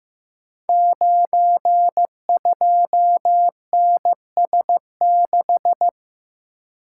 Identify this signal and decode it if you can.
Morse code, heard as a keyed tone, about 15 wpm, 700 Hz. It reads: 92NS6
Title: Code Groups - 5 Letter:Numbers - 15wpm